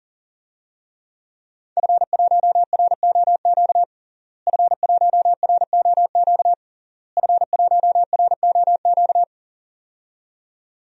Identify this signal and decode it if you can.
Morse code, heard as a keyed tone, about 40 wpm, 700 Hz. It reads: F1ROQ F1ROQ F1ROQ
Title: DXCC-100 Call Signs - 5 Characters - Repeated 3x - 40wpm